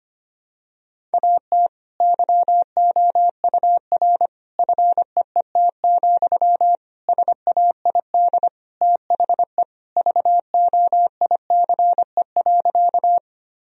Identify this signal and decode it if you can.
Morse code, heard as a keyed tone, about 25 wpm, 700 Hz. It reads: AT YOUR FEET, SAID THE VOICE.
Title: Sentences from Top 1000 Words - Review Part 1 - 25wpm